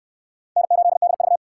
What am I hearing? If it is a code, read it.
Morse code, heard as a keyed tone, about 65 wpm, 700 Hz. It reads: N8DV